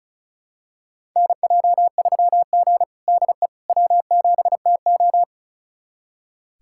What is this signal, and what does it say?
Morse code, heard as a keyed tone, about 36 wpm, 700 Hz. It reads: NJ3G DE W7TO
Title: USA Callsign DE Callsign - 4 Characters - 35wpm